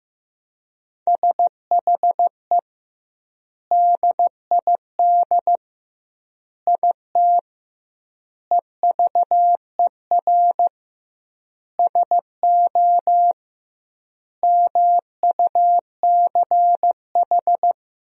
Morse code, heard as a keyed tone, about 15 wpm, 700 Hz. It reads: SHE DID IT EVER SO MUCH
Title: 2x Word-Spacing - Sentences from Top 400 Words - 15wpm